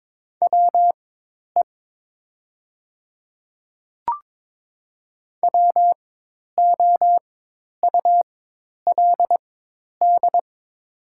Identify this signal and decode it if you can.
Morse code, heard as a keyed tone, about 22 wpm, spaced as for 10 wpm, 700 Hz. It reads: WE WOULD
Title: 4x Character Spacing - Top 100 Words - 22wpm